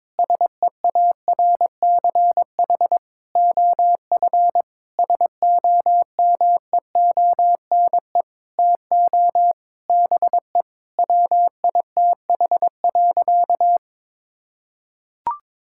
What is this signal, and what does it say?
Morse code, heard as a keyed tone, about 22 wpm, 700 Hz. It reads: SEARCH OF SOMEONE TO BE WITH.
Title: Sentences from Top 1000 Words - Review Part 5 - 22wpm